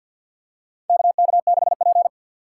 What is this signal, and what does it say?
Morse code, heard as a keyed tone, about 52 wpm, 700 Hz. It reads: KK6P
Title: USA Call Signs - 4 Characters - 3x Character Spacing - 50wpm